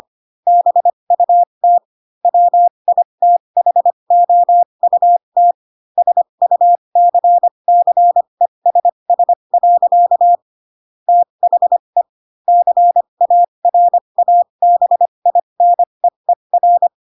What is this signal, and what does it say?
Morse code, heard as a keyed tone, about 25 wpm, spaced as for 22 wpm, 700 Hz. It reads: BUT WITHOUT SUCCESS. THE CARABINEER
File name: Pinocchio / Pinocchio ch3 18 25 ik1hge